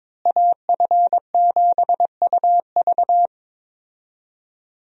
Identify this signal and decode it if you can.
Morse code, heard as a keyed tone, about 22 wpm, 700 Hz. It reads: AF7UV
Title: USA Call Signs - 5 Characters - Repeated 2x - 22wpm